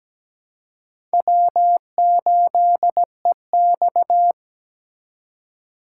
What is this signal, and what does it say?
Morse code, heard as a keyed tone, about 17 wpm, 700 Hz. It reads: W8EX
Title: USA Call Signs - 4 Characters - 2x Character Spacing - 17wpm